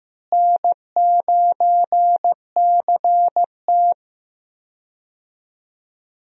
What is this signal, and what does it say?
Morse code, heard as a keyed tone, about 14 wpm, 700 Hz. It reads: N9CT
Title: USA Call Signs - 4 Characters - Repeated 3x - 15wpm